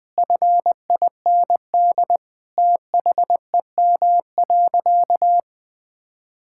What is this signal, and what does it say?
Morse code, heard as a keyed tone, about 21 wpm, 700 Hz. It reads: FIND THEM.
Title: Sentences from Top 1000 Words - Review Part 5 - 20wpm